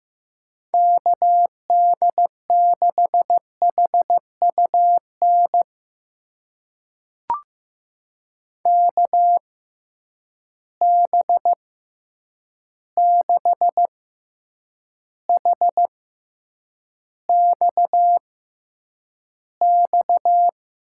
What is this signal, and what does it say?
Morse code, heard as a keyed tone, about 15 wpm, spaced as for 8 wpm, 700 Hz. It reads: KD6HUN KB6HXX
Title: USA Call Signs - 6 Characters - 6x Character Spacing - 15wpm